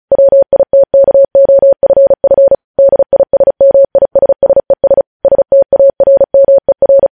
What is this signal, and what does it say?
Morse code, heard as a keyed tone, about 36 wpm, 550 Hz. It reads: WITKOFF DISMISSES STARMER
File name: podcast / News headlines-1x-35x35